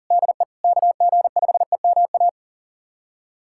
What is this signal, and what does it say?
Morse code, heard as a keyed tone, about 40 wpm, 700 Hz. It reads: DE KG5EMA
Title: USA Callsign DE Callsign - 6 Characters - 40wpm